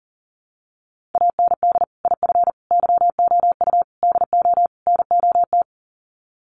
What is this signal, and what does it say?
Morse code, heard as a keyed tone, about 40 wpm, 700 Hz. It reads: AND IF YOU DO NOT
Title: Sets of 5 Words 40wpm